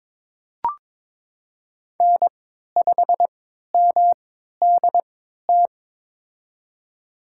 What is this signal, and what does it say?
Morse code, heard as a keyed tone, about 22 wpm, spaced as for 13 wpm, 700 Hz. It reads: N5MDT
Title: USA Call Signs - 5 Characters - 3x Character Spacing - 22wpm